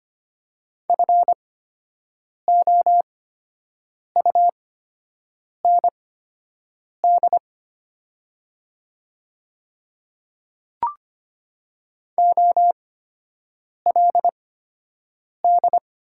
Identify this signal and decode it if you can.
Morse code, heard as a keyed tone, about 25 wpm, spaced as for 7 wpm, 700 Hz. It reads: FOUND OLD
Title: 8x Character Spacing - Top 300 Words - 25wpm